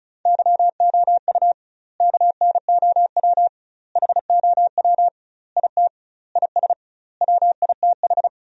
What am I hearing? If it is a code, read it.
Morse code, heard as a keyed tone, about 36 wpm, 700 Hz. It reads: YOU KNOW HOW IT IS WITH
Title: 2x Word-Spacing - Sentences from Top 100 Words - 35wpm